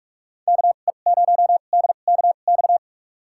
Morse code, heard as a keyed tone, about 41 wpm, spaced as for 31 wpm, 700 Hz. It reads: KE0DKX
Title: USA Call Signs - 6 Characters - 2x Character Spacing - 45wpm